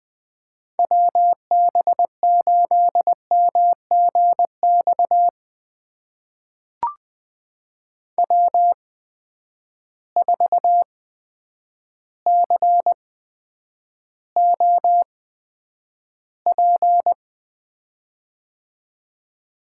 Morse code, heard as a keyed tone, about 21 wpm, spaced as for 9 wpm, 700 Hz. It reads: WB8MGX W4COP
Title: USA Call Signs - 8x Character Spacing - 20wpm